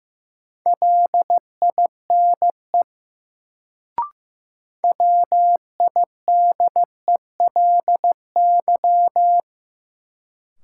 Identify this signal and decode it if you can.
Morse code, heard as a keyed tone, about 15 wpm, 700 Hz. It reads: LINE WIDELY
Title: Sentences from Top 1200 Words - New Word Drill Alt - 15wpm